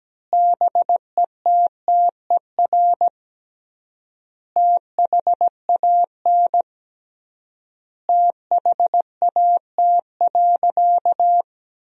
Morse code, heard as a keyed tone, about 17 wpm, 700 Hz. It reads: BETTER THAN THAT.
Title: 3x Word-Spacing - Sentences from Top 400 Words - 17wpm